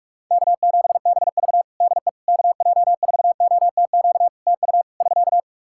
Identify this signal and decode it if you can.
Morse code, heard as a keyed tone, about 44 wpm, 700 Hz. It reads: K7BV DE KJ4OTQ TU <SK>
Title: Practice QSO - Exchange 4 - Ending - 45wpm